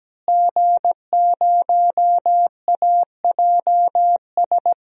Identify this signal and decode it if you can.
Morse code, heard as a keyed tone, about 17 wpm, 700 Hz. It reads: G0AJS
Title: DXCC-100 Call Signs - 5 Characters - Repeated 3x - 17wpm